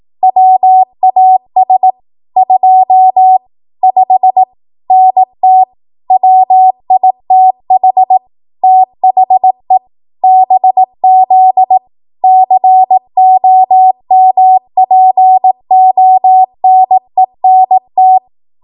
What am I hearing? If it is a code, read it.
Morse code, heard as a keyed tone, about 18 wpm, 750 Hz. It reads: WAS 25 NT WITH THE BZ COMPONENT
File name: Morse / 18WPM